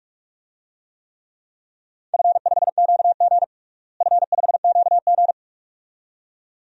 Morse code, heard as a keyed tone, about 46 wpm, 700 Hz. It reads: F5QG F5QG
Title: DXCC-100 Call Signs - 4 Characters - Repeated 2x - 45wpm